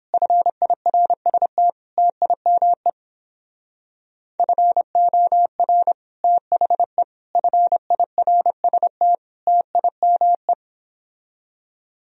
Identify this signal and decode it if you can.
Morse code, heard as a keyed tone, about 29 wpm, 700 Hz. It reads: FIRST TIME FOR THE FIRST TIME